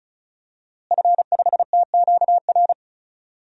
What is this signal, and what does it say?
Morse code, heard as a keyed tone, about 36 wpm, 700 Hz. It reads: F5TQR
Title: DXCC-100 Call Signs - 5 Characters - Repeated 3x - 35wpm